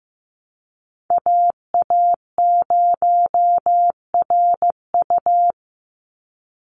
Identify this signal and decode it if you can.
Morse code, heard as a keyed tone, about 15 wpm, 700 Hz. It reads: AA0RU
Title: USA Call Signs 15wpm